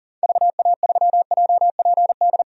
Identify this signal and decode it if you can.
Morse code, heard as a keyed tone, about 40 wpm, 700 Hz. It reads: VA3JPD
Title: DXCC-100 Call Signs - 6 Characters - 40wpm